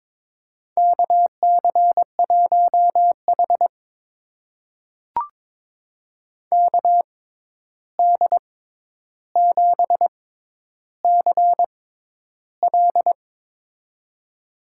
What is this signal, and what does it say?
Morse code, heard as a keyed tone, about 22 wpm, spaced as for 11 wpm, 700 Hz. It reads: KC1H KD7CL
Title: USA Call Signs - 6x Character Spacing - 22wpm